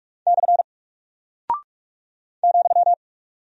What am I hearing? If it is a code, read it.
Morse code, heard as a keyed tone, about 44 wpm, 700 Hz. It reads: / ,
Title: Single Character (Letter:Number:Punctuation:Prosign) Alt 45wpm